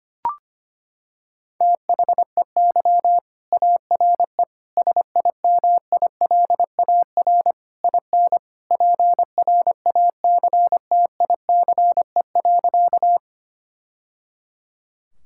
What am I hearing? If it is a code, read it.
Morse code, heard as a keyed tone, about 25 wpm, 700 Hz. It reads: THEY ARE SIMILAR IN PRACTICE.
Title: Sentences from Top 1000 Words - Review Part 5 - 25wpm